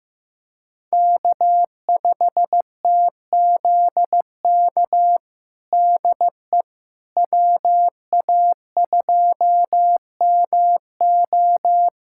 Morse code, heard as a keyed tone, about 16 wpm, 700 Hz. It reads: K5TZK DE WA2MO
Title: USA Callsign DE Callsign - 5 Characters - 15wpm